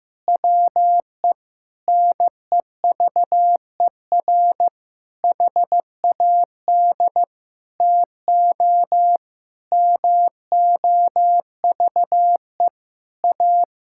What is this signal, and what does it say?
Morse code, heard as a keyed tone, about 16 wpm, 700 Hz. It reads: WE NEVER HAD TO MOVE A
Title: Sentences from Top 1600 Words 15wpm